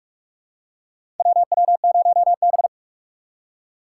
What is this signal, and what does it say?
Morse code, heard as a keyed tone, about 42 wpm, 700 Hz. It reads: WW0B
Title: USA Call Signs - 4 Characters - 4x Character Spacing - 45wpm